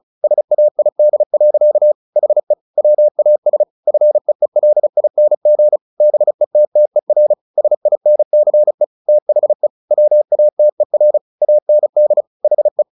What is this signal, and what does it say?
Morse code, heard as a keyed tone, about 35 wpm, 600 Hz. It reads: SAID. HE WAS FEELING BETTER SINCE THE WATER AND HE
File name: the old man and the sea 2 35wpm